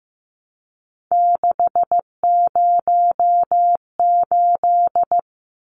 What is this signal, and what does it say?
Morse code, heard as a keyed tone, about 15 wpm, 700 Hz. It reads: 608
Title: Numbers - 3-Digit 15wpm